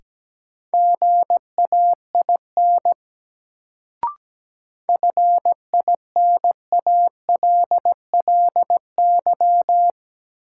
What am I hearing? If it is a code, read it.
Morse code, heard as a keyed tone, about 17 wpm, 700 Hz. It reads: GAIN FINALLY
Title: Sentences from Top 1000 Words - New Word Drill - 17wpm